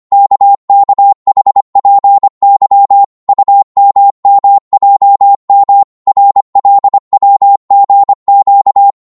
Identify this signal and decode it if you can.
Morse code, heard as a keyed tone, about 25 wpm, 800 Hz. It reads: KKHPY UMMJM RLWGQ
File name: Morse / ltr-25wpm